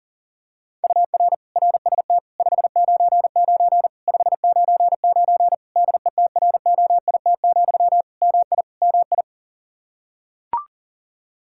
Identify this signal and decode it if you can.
Morse code, heard as a keyed tone, about 40 wpm, 700 Hz. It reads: UR RST 599 599 DETROIT, MI MI
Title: Practice QSO - Exchange 1 - Beginning - 40wpm